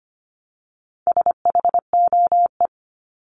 Morse code, heard as a keyed tone, about 25 wpm, 700 Hz. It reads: SHOE
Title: Top 1751-2000 Words 25wpm